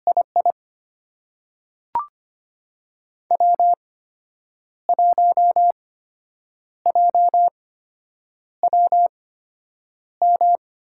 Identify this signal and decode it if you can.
Morse code, heard as a keyed tone, about 25 wpm, spaced as for 8 wpm, 700 Hz. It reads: II W1JWM
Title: USA Call Signs - 8x Character Spacing - 25wpm